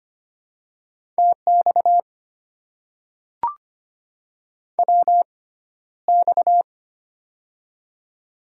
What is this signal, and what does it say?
Morse code, heard as a keyed tone, about 25 wpm, spaced as for 13 wpm, 700 Hz. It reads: TX WX
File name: Top 90 QSO Elements - 6x Character Spacing - 25wpm